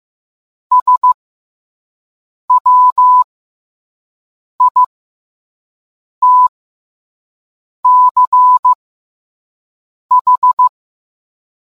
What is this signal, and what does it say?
Morse code, heard as a keyed tone, about 15 wpm, spaced as for 5 wpm, 1000 Hz. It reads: SWITCH